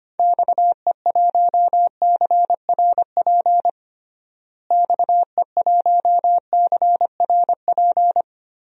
Morse code, heard as a keyed tone, about 25 wpm, 700 Hz. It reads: XE1CRP XE1CRP
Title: DXCC-100 Call Signs - 6 Characters - Repeated 3x - 25wpm